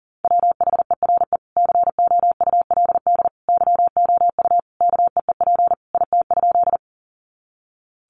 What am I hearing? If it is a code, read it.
Morse code, heard as a keyed tone, about 40 wpm, 700 Hz. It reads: WHERE COULD YOU KEEP IT?
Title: Sentences from Top 300 Words 40wpm